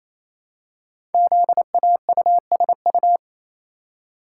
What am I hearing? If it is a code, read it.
Morse code, heard as a keyed tone, about 29 wpm, 700 Hz. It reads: ZAUSU
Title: Code Groups - 5 Letters - 28wpm